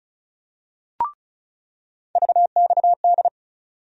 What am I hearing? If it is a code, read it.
Morse code, heard as a keyed tone, about 36 wpm, 700 Hz. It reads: VXD
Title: USA Call Sign Postfixes - 3 Letters 35wpm